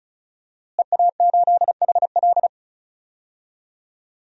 Morse code, heard as a keyed tone, about 34 wpm, 700 Hz. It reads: EA8HL
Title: DXCC-100 Call Signs - 5 Characters - Repeated 2x - 35wpm